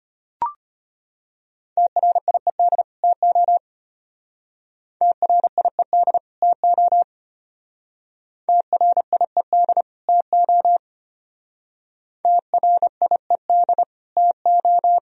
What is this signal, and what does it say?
Morse code, heard as a keyed tone, about 34 wpm, 700 Hz. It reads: TRIED TO TRIED TO TRIED TO TRIED TO